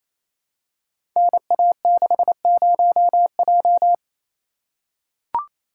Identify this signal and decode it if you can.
Morse code, heard as a keyed tone, about 28 wpm, 700 Hz. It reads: NA60J
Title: Code Groups - 5 Letter:Numbers - 28wpm